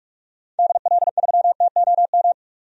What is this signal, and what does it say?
Morse code, heard as a keyed tone, about 43 wpm, 700 Hz. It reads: DL3TOM
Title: DXCC-100 Call Signs - 6 Characters - 45wpm